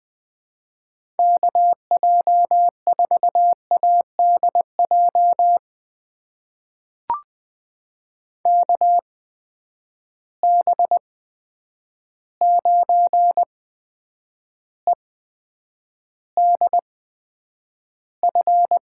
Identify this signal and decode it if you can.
Morse code, heard as a keyed tone, about 20 wpm, spaced as for 9 wpm, 700 Hz. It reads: KJ4ADJ KB9EDF
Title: USA Call Signs - 6 Characters - 8x Character Spacing - 20wpm